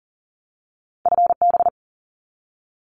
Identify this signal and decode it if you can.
Morse code, heard as a keyed tone, about 40 wpm, 700 Hz. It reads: FB
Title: Abbreviations 40wpm